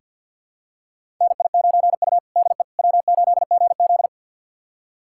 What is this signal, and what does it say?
Morse code, heard as a keyed tone, about 48 wpm, 700 Hz. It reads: NI9U DE W8GZ
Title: USA Callsign DE Callsign - 4 Characters - 50wpm